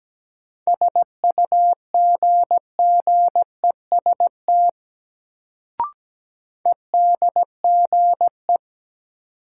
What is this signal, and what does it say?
Morse code, heard as a keyed tone, about 17 wpm, 700 Hz. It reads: SUGGEST EDGE
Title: Sentences from Top 1300 Words - New Word Drill - 17wpm